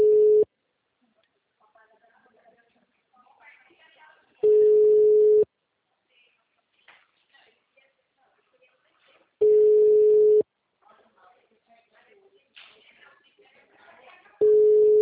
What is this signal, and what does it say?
Ringback tone (Europe, 425 Hz), with speech.